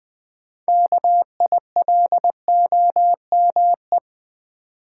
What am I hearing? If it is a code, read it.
Morse code, heard as a keyed tone, about 20 wpm, 700 Hz. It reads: KILOME
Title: Build A Word - 3 Letters At A Time - 9-Letter Words - 20wpm